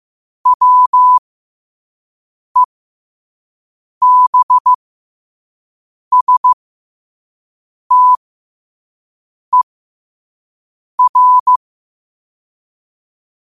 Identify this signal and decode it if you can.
Morse code, heard as a keyed tone, about 15 wpm, spaced as for 5 wpm, 1000 Hz. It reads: WEBSTER
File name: qotdfiles / 5wpm